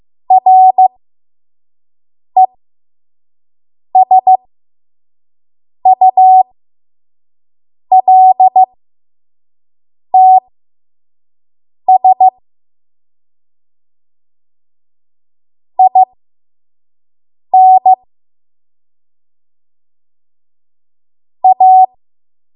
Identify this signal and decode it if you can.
Morse code, heard as a keyed tone, about 15 wpm, spaced as for 5 wpm, 750 Hz. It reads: RESULTS IN A
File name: Morse / 05WPM